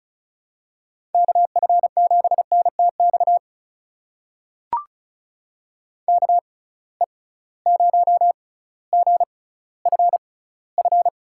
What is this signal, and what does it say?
Morse code, heard as a keyed tone, about 36 wpm, spaced as for 18 wpm, 700 Hz. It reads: KF7NTX KE0GFF
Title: USA Call Signs - 6 Characters - 6x Character Spacing - 35wpm